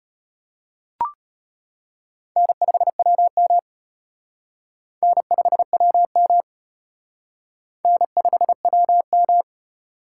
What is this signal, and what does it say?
Morse code, heard as a keyed tone, about 31 wpm, 700 Hz. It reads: N5WM N5WM N5WM
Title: Speed Racing - USA 1x2 Call Signs 25wpm